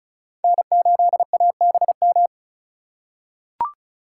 Morse code, heard as a keyed tone, about 36 wpm, 700 Hz. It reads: N8ABM
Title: USA Call Signs - 5 Characters - 2x Character Spacing - 35wpm